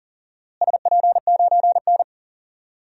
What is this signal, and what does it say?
Morse code, heard as a keyed tone, about 40 wpm, 700 Hz. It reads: SP9N
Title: DXCC-100 Call Signs - 4 Characters - Repeated 2x - 40wpm